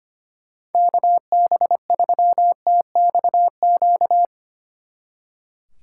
Morse code, heard as a keyed tone, about 32 wpm, 700 Hz. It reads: KB3TXQ